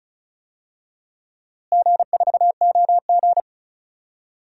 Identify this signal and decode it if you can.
Morse code, heard as a keyed tone, about 36 wpm, 700 Hz. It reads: G4OG
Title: DXCC-100 Call Signs - 4 Characters - Repeated 3x - 35wpm